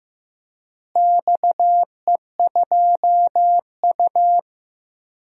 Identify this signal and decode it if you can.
Morse code, heard as a keyed tone, about 15 wpm, 700 Hz. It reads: XE2U
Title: DXCC-100 Call Signs - 4 Characters - Repeated 3x - 15wpm